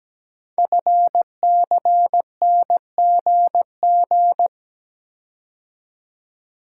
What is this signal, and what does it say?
Morse code, heard as a keyed tone, about 17 wpm, 700 Hz. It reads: FCNGG
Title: Code Groups - 5 Letters - 17wpm